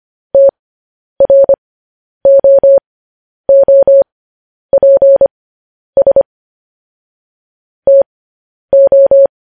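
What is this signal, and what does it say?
Morse code, heard as a keyed tone, about 25 wpm, spaced as for 10 wpm, 550 Hz. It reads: TROOPS TO
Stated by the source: Morse code transmission of news headlines.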